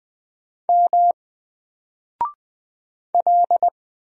Morse code, heard as a keyed tone, about 20 wpm, 700 Hz. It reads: M L
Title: Single Character (Letter:Number:Punctuation:Prosign) 20wpm